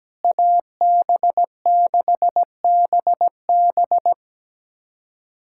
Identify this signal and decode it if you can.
Morse code, heard as a keyed tone, about 17 wpm, 700 Hz. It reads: AB6BB
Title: USA Callsign DE Callsign - 5 Characters - 17wpm